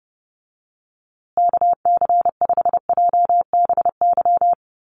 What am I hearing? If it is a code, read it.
Morse code, heard as a keyed tone, about 30 wpm, 700 Hz. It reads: KC5JBY
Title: USA Call Signs Repeated 3x 30wpm